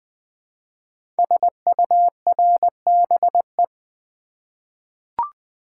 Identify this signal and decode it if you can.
Morse code, heard as a keyed tone, about 20 wpm, 700 Hz. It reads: SURBE
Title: Code Groups - 5 Letters - 20wpm